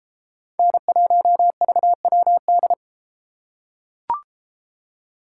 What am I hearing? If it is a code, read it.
Morse code, heard as a keyed tone, about 33 wpm, 700 Hz. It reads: N1VWD